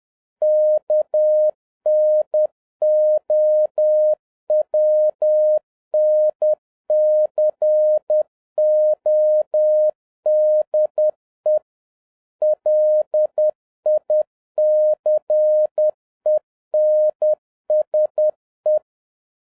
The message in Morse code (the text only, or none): KNOWN CODE LICENSE